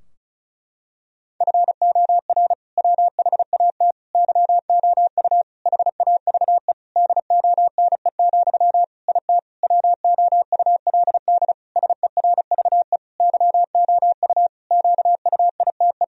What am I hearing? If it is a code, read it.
Morse code, heard as a keyed tone, about 36 wpm, 700 Hz. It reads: FOR WHAT YOU HAVE DONE, IT WOULD SERVE YOU QUITE
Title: Sentences from Top 1000 Words - Review Part 2 - 35wpm